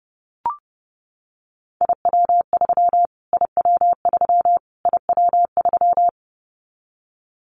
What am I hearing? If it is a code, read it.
Morse code, heard as a keyed tone, about 30 wpm, 700 Hz. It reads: IW3 IW3 IW3
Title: Callsign Prefixes for Top 25 DXCC - Repeated 3x - 30wpm